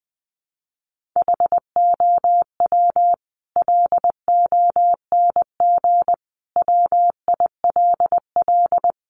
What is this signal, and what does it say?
Morse code, heard as a keyed tone, about 20 wpm, 700 Hz. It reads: HOW LONG WILL
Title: Sentences from Top 1200 Words 20wpm